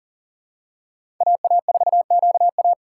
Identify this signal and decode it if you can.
Morse code, heard as a keyed tone, about 40 wpm, 700 Hz. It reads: AA4QA
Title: USA Call Signs - 5 Characters - 6x Character Spacing - 40wpm